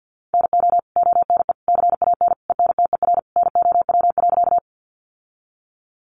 Morse code, heard as a keyed tone, about 50 wpm, 700 Hz. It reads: NO ONE CAN ENTER NOW.
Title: Sentences from Top 1300 Words 50wpm